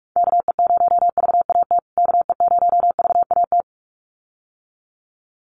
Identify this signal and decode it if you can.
Morse code, heard as a keyed tone, about 44 wpm, 700 Hz. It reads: KE0VAT KE0VAT
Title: USA Call Signs Repeated 2x 45wpm